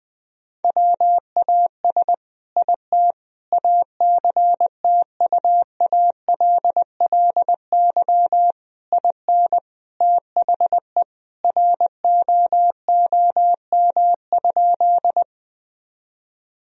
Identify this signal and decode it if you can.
Morse code, heard as a keyed tone, about 20 wpm, 700 Hz. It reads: WAS IT ACTUALLY IN THE ROOM?
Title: Sentences from Top 1000 Words - Review Part 4 - 20wpm